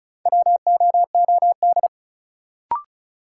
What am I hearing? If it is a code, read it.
Morse code, heard as a keyed tone, about 36 wpm, 700 Hz. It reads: WOOD
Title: Sentences from Top 1000 Words - Review Part 2 - New Word Drill - 35wpm